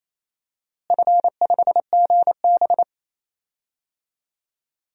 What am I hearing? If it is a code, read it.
Morse code, heard as a keyed tone, about 28 wpm, 700 Hz. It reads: F5GB
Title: DXCC-100 Call Signs - 4 Characters - Repeated 3x - 28wpm